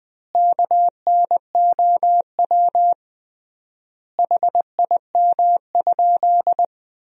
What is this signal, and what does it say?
Morse code, heard as a keyed tone, about 21 wpm, 700 Hz. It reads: KNOW HIM?
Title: 3x Word-Spacing - Sentences from Top 100 Words - 20wpm